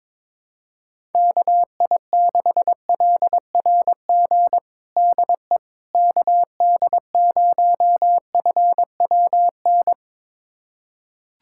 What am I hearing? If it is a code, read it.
Morse code, heard as a keyed tone, about 21 wpm, 700 Hz. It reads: KI6LRG DE KD0FWN
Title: USA Callsign DE Callsign - 6 Characters - 22wpm